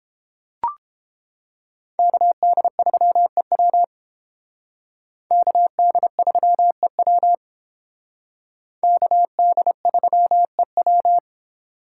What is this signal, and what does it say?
Morse code, heard as a keyed tone, about 30 wpm, 700 Hz. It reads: KD3EW KD3EW KD3EW